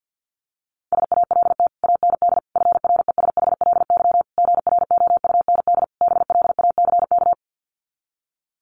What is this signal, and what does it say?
Morse code, heard as a keyed tone, about 50 wpm, 700 Hz. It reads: SALT AND FRESHLY GROUND BLACK
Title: Sets of 5 Words Encore 50wpm